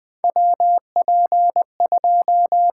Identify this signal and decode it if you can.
Morse code, heard as a keyed tone, about 20 wpm, 700 Hz. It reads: WP2
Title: USA Call Signs - 6 Characters - 6x Character Spacing - 20wpm